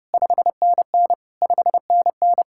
Morse code, heard as a keyed tone, about 39 wpm, 700 Hz. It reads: 5NN 5NN